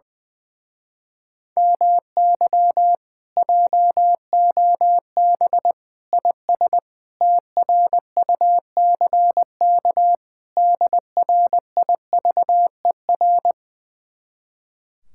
Morse code, heard as a keyed tone, about 20 wpm, 700 Hz. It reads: MY JOB IS TRUCK DRIVER
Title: Speed-Racing Common QSO Phrases Encore - 20wpm